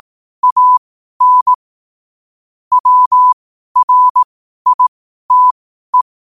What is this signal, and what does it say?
Morse code, heard as a keyed tone, about 18 wpm, spaced as for 12 wpm, 1000 Hz. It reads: AN WRITE